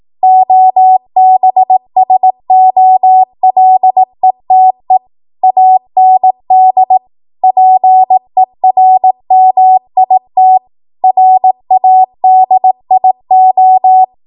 Morse code, heard as a keyed tone, about 18 wpm, 750 Hz. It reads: OBSOLETE AND PERMIT RADIO
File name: Morse / 18WPM